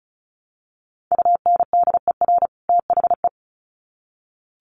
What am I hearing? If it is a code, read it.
Morse code, heard as a keyed tone, about 34 wpm, 700 Hz. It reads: UNDER THE
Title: Sets of 2 Words 35wpm